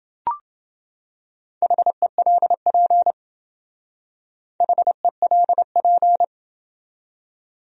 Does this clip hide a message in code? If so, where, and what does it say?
Yes: Morse code from 1.6 to 6.2 s, heard as a keyed tone, about 28 wpm, 700 Hz. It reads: HELP HELP